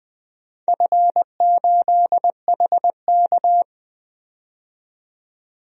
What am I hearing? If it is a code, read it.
Morse code, heard as a keyed tone, about 20 wpm, 700 Hz. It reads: F8HK
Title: DXCC-100 Call Signs - 4 Characters - 20wpm